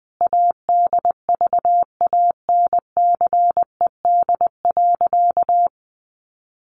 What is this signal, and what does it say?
Morse code, heard as a keyed tone, about 20 wpm, 700 Hz. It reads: ADVANCED.
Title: Sentences from Top 1200 Words 20wpm